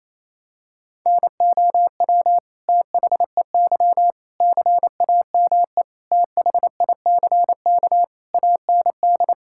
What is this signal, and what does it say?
Morse code, heard as a keyed tone, about 28 wpm, 700 Hz. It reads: NOW THEY CAME THICK AND
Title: Sentences from Top 1600 Words 28wpm